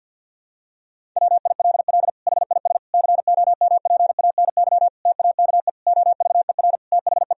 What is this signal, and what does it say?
Morse code, heard as a keyed tone, about 50 wpm, 700 Hz. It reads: WILL HIS COMPANY TAKE OVER THE
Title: Sentences from Top 1000 Words - Review Part 4 - 50wpm